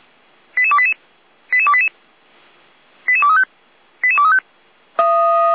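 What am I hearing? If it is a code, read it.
Five-tone selective call (ZVEI), heard as quick runs of five tones, tones of about 70 ms. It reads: 88288 88288 88235 88235
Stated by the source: ZVEI five-tone selective call 88289 received with an SDR.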